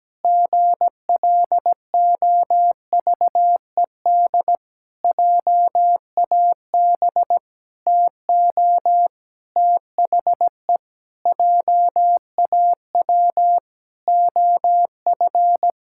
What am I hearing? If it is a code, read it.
Morse code, heard as a keyed tone, about 17 wpm, 700 Hz. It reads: GLOVED JAB TO THE JAW OF
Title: Pangram Challenge v2 - 17wpm